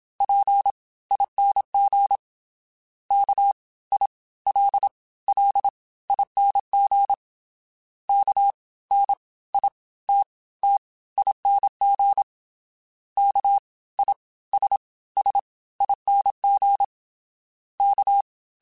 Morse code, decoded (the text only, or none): PING KILLING KNITTING KISSING K